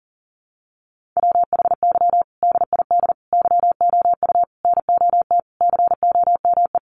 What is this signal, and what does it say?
Morse code, heard as a keyed tone, about 37 wpm, 700 Hz. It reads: WHY DID YOU NOT COME
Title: Sentences from Top 1500 Words 40wpm